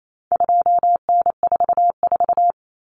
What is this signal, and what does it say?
Morse code, heard as a keyed tone, about 28 wpm, 700 Hz. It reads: 2N44
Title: Common Cut Numbers - 4-Digit 28wpm